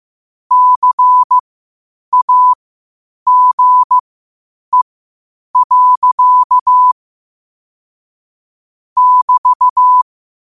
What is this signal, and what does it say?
Morse code, heard as a keyed tone, about 15 wpm, spaced as for 8 wpm, 1000 Hz. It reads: CAGE. <BT>
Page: Quotes for Sat, 16 Aug 2025 in Morse Code at 8 words per minute.